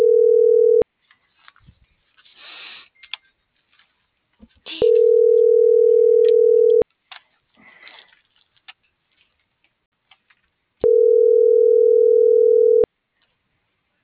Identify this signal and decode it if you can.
Ringback tone (North America), 440 + 480 Hz.